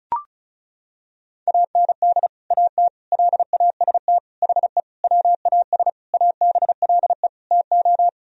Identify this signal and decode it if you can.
Morse code, heard as a keyed tone, about 36 wpm, 700 Hz. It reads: AND AT LAST HE WAS ABLE TO
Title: Sentences from Top 1000 Words - Review Part 2 - 35wpm